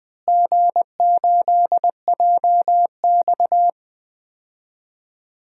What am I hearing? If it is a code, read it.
Morse code, heard as a keyed tone, about 20 wpm, 700 Hz. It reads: G8JX
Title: DXCC-100 Call Signs - 4 Characters - Repeated 2x - 20wpm